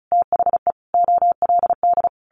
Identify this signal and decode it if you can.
Morse code, heard as a keyed tone, about 34 wpm, 700 Hz. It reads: THE OLD
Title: Sentences from Top 1100 Words 35wpm